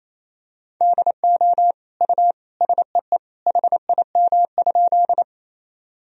Morse code, heard as a keyed tone, about 28 wpm, 700 Hz. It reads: DO U SEE HIM?
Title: Sentences from Top 100 Words and Abbreviations 28wpm